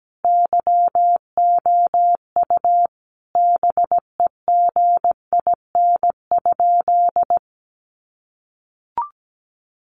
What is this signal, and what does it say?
Morse code, heard as a keyed tone, about 17 wpm, 700 Hz. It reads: YOU BEGIN?
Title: Sentences from Top 300 Words 17wpm